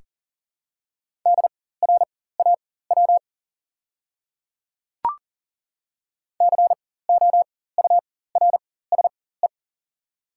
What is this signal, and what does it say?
Morse code, heard as a keyed tone, about 40 wpm, spaced as for 19 wpm, 700 Hz. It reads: DRAW COURSE
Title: 4x Character Spacing - Top 500 Words - 40wpm